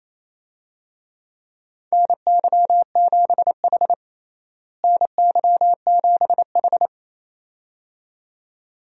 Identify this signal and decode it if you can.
Morse code, heard as a keyed tone, about 28 wpm, 700 Hz. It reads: NY7H NY7H
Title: USA Call Signs - 4 Characters - Repeated 2x - 28wpm